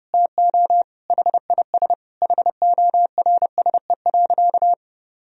Morse code, heard as a keyed tone, about 30 wpm, 700 Hz. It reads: TO HIS HORSE.
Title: Sentences from Top 1000 Words - Review Part 1 - 30wpm